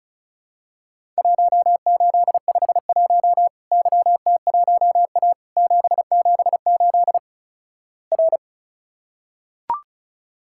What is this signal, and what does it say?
Morse code, heard as a keyed tone, about 36 wpm, 700 Hz. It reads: 1851 YT1A 778 R
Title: Worked All Europe DX Contest - QTC Exchange - 35wpm